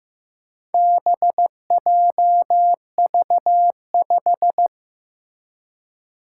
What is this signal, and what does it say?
Morse code, heard as a keyed tone, about 15 wpm, 700 Hz. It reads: BJV5
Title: Four Challenging Letter-Number Combinations 15wpm